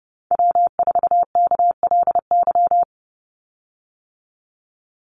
Morse code, heard as a keyed tone, about 30 wpm, 700 Hz. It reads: W4KLY
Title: Call Signs - 30wpm